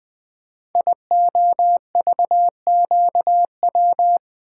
Morse code, heard as a keyed tone, about 20 wpm, 700 Hz. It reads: IOVQW
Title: Code Groups - 5 Letter:Numbers - 20wpm